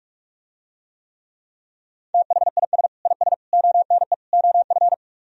Morse code, heard as a keyed tone, about 46 wpm, 700 Hz. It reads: THIS IS ONE OF